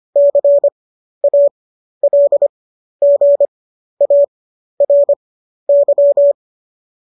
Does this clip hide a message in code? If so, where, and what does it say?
Yes: Morse code from 0.2 to 6.3 s, heard as a keyed tone, about 25 wpm, spaced as for 12 wpm, 570 Hz. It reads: CALGARY